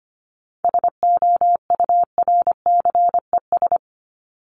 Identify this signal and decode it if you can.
Morse code, heard as a keyed tone, about 25 wpm, 700 Hz. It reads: SOURCES
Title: ES:S Suffix - Top 100 Words with Base Words 25wpm